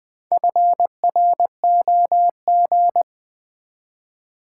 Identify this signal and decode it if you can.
Morse code, heard as a keyed tone, about 20 wpm, 700 Hz. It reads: FROG
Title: Build A Word - 1 Letter At A Time - 3 to 5-Letter Words - 20wpm